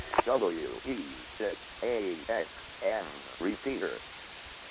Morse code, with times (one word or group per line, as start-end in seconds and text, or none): none